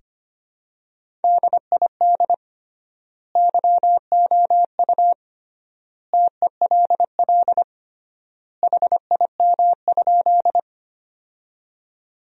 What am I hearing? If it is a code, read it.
Morse code, heard as a keyed tone, about 25 wpm, 700 Hz. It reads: DID YOU TELL HIM?
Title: 3x Word-Spacing - Sentences from Top 200 Words - 25wpm